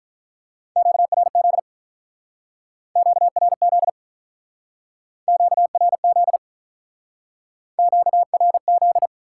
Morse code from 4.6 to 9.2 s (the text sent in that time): QRZ QRZ